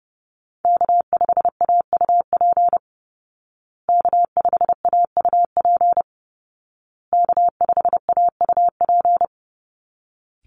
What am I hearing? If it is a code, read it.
Morse code, heard as a keyed tone, about 30 wpm, 700 Hz. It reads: K5AUP K5AUP K5AUP
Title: Call Signs Repeated 2x - 30wpm